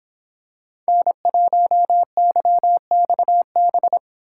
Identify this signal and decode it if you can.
Morse code, heard as a keyed tone, about 24 wpm, 700 Hz. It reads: N1YXB